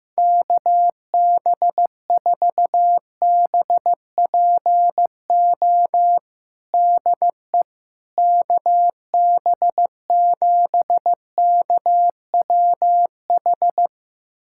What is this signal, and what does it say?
Morse code, heard as a keyed tone, about 15 wpm, 700 Hz. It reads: KB4BPO DE KB7KWH
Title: USA Callsign DE Callsign - 6 Characters - 15wpm